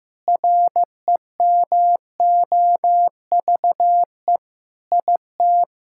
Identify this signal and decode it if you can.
Morse code, heard as a keyed tone, about 15 wpm, 700 Hz. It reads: REMOVE IT
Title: Sentences from Top 1000 Words - Review Part 5 - 15wpm